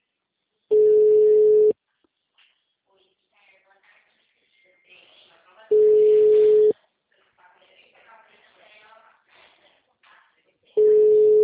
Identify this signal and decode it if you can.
Ringback tone (Europe, 425 Hz), with speech.
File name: out- -2023- - - .74571